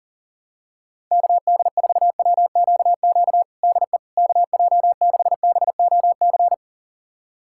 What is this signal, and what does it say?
Morse code, heard as a keyed tone, about 40 wpm, 690 Hz. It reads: KD4WQQ DE KJ6BOC
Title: USA Callsign DE Callsign - 6 Characters - 40wpm